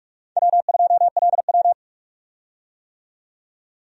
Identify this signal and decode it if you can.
Morse code, heard as a keyed tone, about 41 wpm, 700 Hz. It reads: W2LW
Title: USA Call Signs - 4 Characters - 45wpm